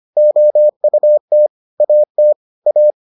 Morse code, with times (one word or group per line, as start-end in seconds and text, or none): 0.2-1.5 OUT
1.8-2.3 AT
2.7-2.9 A